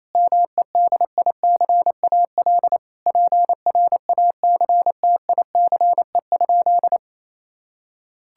Morse code, heard as a keyed tone, about 28 wpm, 700 Hz. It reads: MEDICAL PRACTICE?
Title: Sentences from Top 1000 Words - Review Part 5 - 28wpm